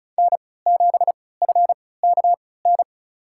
Morse code, heard as a keyed tone, about 34 wpm, spaced as for 20 wpm, 700 Hz. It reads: N7FKN
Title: USA Call Signs - 5 Characters - 3x Character Spacing - 35wpm